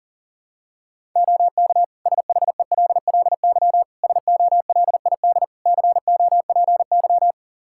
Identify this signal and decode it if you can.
Morse code, heard as a keyed tone, about 40 wpm, 700 Hz. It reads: OK SHELLY SOLID COPY
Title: Practice QSO - Exchange 3 - More Info - 40wpm